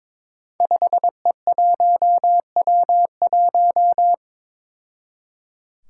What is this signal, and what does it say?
Morse code, heard as a keyed tone, about 22 wpm, 700 Hz. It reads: 5E1W1
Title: Five Challenging Letter-Number Combinations 22wpm